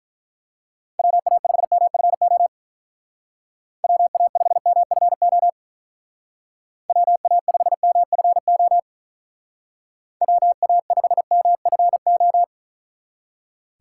Morse code, heard as a keyed tone, about 44 wpm, 700 Hz. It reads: WA5MFO WA5MFO WA5MFO WA5MFO